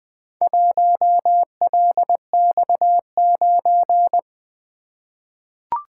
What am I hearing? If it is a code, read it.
Morse code, heard as a keyed tone, about 20 wpm, 700 Hz. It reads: 1LX9
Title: Four Letter-Number Combinations - 20wpm